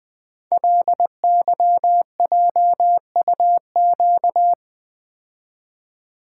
Morse code, heard as a keyed tone, about 20 wpm, 700 Hz. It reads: LYJUQ
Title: Code Groups - 5 Letters - 20wpm